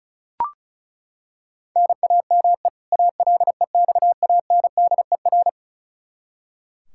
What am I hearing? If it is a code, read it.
Morse code, heard as a keyed tone, about 34 wpm, 700 Hz. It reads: NAME ALEXANDER
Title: Practice QSO - Exchange 1 - Beginning - 35wpm